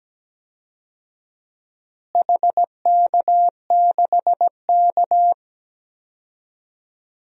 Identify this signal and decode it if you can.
Morse code, heard as a keyed tone, about 17 wpm, 700 Hz. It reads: HK6K
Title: DXCC-100 Call Signs - 4 Characters - Repeated 2x - 17wpm